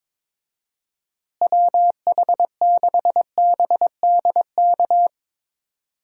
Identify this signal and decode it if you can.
Morse code, heard as a keyed tone, about 22 wpm, 700 Hz. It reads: WH6BDK
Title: USA Call Signs - 6 Characters - Repeated 2x - 22wpm